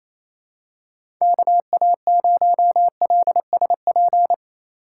Morse code, heard as a keyed tone, about 28 wpm, 700 Hz. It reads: KA0LSP
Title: USA Call Signs - 6 Characters - 4x Character Spacing - 28wpm